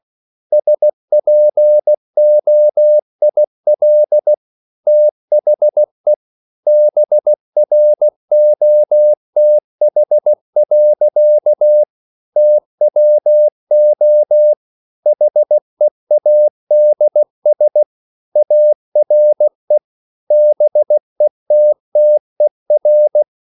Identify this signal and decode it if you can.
Morse code, heard as a keyed tone, about 16 wpm, 600 Hz. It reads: SPOIL THE BROTH. TWO HEADS ARE BETTER
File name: CW / Practice / EnglishSayings16